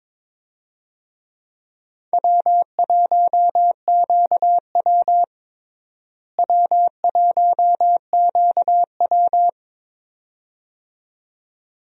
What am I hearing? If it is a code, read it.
Morse code, heard as a keyed tone, about 21 wpm, 700 Hz. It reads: W1QW W1QW
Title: USA Call Signs - 4 Characters - Repeated 2x - 22wpm